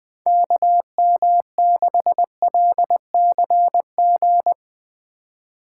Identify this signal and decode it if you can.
Morse code, heard as a keyed tone, about 22 wpm, 700 Hz. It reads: KM6LCG
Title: Speed Racing - USA 2x3 Call Signs 20wpm